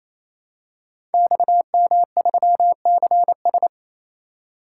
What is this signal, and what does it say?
Morse code, heard as a keyed tone, about 27 wpm, 700 Hz. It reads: XM3CS
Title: Code Groups - 5 Letter:Numbers - 28wpm